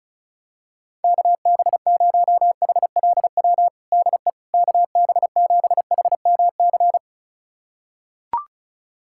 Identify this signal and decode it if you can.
Morse code, heard as a keyed tone, about 36 wpm, 700 Hz. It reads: KB0HLW DE KB7HMC
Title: USA Callsign DE Callsign - 6 Characters - 35wpm